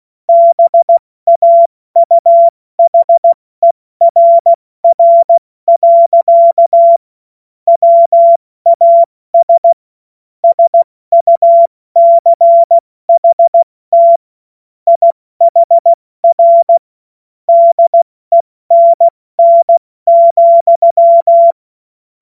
Morse code, heard as a keyed tone, about 16 wpm, 680 Hz. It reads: BAUHERR. WAS SUCHT IHR DENN,
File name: Morsekurs / Witze / Witz 20. Januar 2026 70 F80